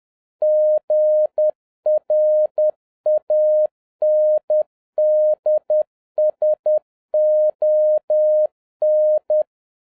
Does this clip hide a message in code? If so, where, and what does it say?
Yes: Morse code from 0.4 to 9.4 s, heard as a keyed tone, about 10 wpm, 610 Hz. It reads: GRANDSON